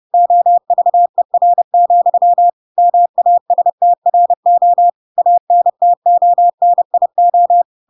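Morse code, heard as a keyed tone, about 30 wpm, 700 Hz. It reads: OVER, MASTRO ANTONIO
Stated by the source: Chapter 2 of The Adventures of Pinocchio read as a book in Morse code (CW) at 30 wpm, by ik1hge.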